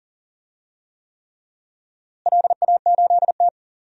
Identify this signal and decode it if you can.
Morse code, heard as a keyed tone, about 40 wpm, 700 Hz. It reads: LA8T
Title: DXCC-100 Call Signs - 4 Characters - Repeated 3x - 40wpm